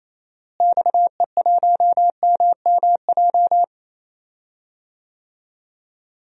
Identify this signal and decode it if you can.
Morse code, heard as a keyed tone, about 27 wpm, 700 Hz. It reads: XE1MMJ
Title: DXCC-100 Call Signs - 6 Characters - Repeated 2x - 28wpm